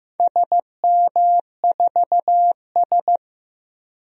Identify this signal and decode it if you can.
Morse code, heard as a keyed tone, about 15 wpm, 700 Hz. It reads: SM4S
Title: DXCC-100 Call Signs - 4 Characters - 15wpm